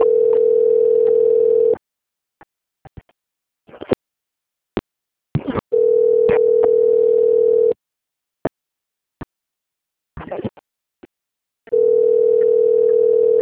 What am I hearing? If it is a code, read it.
Ringback tone (North America), 440 + 480 Hz.
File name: out- - - - - .34678